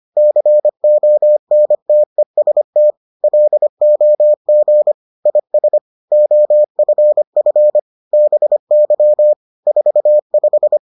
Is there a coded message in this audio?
Yes: Morse code from 0.2 to 10.8 s, heard as a keyed tone, about 25 wpm, 600 Hz. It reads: CONTEST LOG IS OFF BY 45